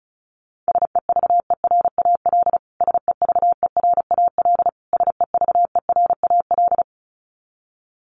Morse code, heard as a keyed tone, about 34 wpm, 700 Hz. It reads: SEVERAL SEVERAL SEVERAL
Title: Top 500 Words Repeated 35wpm